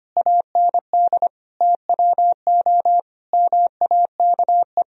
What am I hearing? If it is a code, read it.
Morse code, heard as a keyed tone, about 25 wpm, 700 Hz. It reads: AND TWO MAKE
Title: Sentences from Top 1000 Words - Review Part 1 - 25wpm